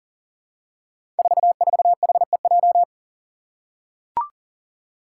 Morse code, heard as a keyed tone, about 40 wpm, 700 Hz. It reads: 44HEJ
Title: Five Challenging Letter-Number Combinations 40wpm